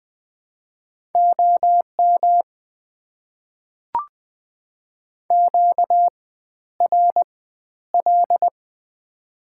Morse code, heard as a keyed tone, about 20 wpm, spaced as for 11 wpm, 690 Hz. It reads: OM QRL
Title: Top 90 QSO Elements - 4x Character Spacing - 20wpm